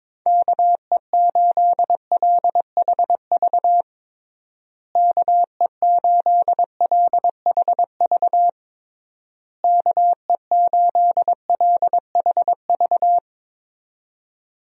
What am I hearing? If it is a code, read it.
Morse code, heard as a keyed tone, about 22 wpm, 700 Hz. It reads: KE8LHV KE8LHV KE8LHV
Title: USA Call Signs - 6 Characters - Repeated 3x - 22wpm